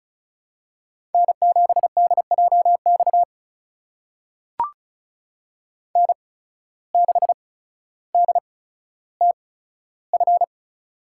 Morse code, heard as a keyed tone, about 34 wpm, spaced as for 15 wpm, 700 Hz. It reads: N7DJX N6DTF
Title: USA Call Signs - 8x Character Spacing - 35wpm